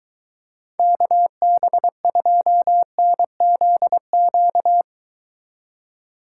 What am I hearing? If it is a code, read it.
Morse code, heard as a keyed tone, about 22 wpm, 700 Hz. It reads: KB2NZQ